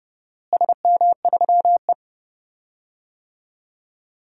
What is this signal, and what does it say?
Morse code, heard as a keyed tone, about 30 wpm, 700 Hz. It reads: SM3E
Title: DXCC-100 Call Signs - 4 Characters - Repeated 3x - 30wpm